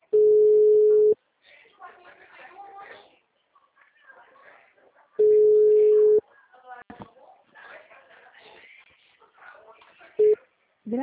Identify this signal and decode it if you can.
Ringback tone (Europe, 425 Hz), with speech.